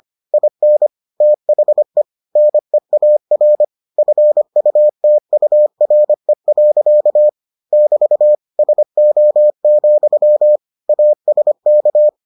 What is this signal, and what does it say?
Morse code, heard as a keyed tone, about 25 wpm, 600 Hz. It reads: IN THE NEAR FUTURE. <BT> SO, ASK